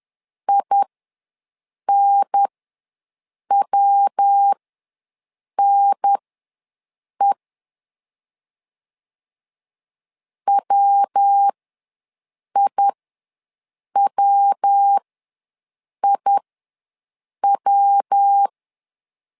Morse code, heard as a keyed tone, about 10 wpm, spaced as for 6 wpm, 800 Hz. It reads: INWNE WIWIW
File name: LEZIONI / 3 MWRO / 3 MW ws 10wpm 2213